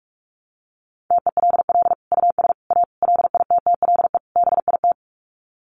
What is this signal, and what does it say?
Morse code, heard as a keyed tone, about 44 wpm, 700 Hz. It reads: TELL US A LITTLE BIT
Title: Sets of 5 Words Encore2 45wpm